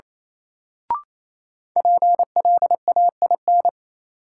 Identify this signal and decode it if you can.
Morse code, heard as a keyed tone, about 28 wpm, 700 Hz. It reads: PLAIN
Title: Sentences from Top 400 Words - New Word Drill - 28wpm